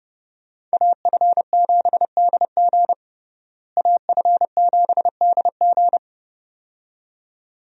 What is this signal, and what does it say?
Morse code, heard as a keyed tone, about 32 wpm, 700 Hz. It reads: AF7DG AF7DG
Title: USA Call Signs - 5 Characters - Repeated 2x - 30wpm